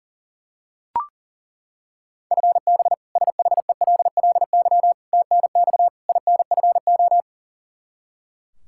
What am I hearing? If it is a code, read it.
Morse code, heard as a keyed tone, about 40 wpm, 700 Hz. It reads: FB SHELLY TNX INFO
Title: K1USN SST QSO Practice 40wpm